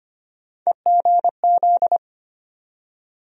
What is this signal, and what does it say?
Morse code, heard as a keyed tone, about 25 wpm, 700 Hz. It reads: EGZ
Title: USA Call Sign Postfixes - 3 Letters 25wpm